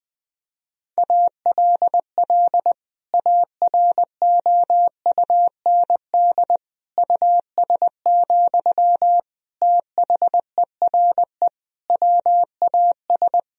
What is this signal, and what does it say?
Morse code, heard as a keyed tone, about 21 wpm, 700 Hz. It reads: ALL AROUND US, THERE WAS
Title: Sentences from Top 1000 Words - Review Part 3 - 20wpm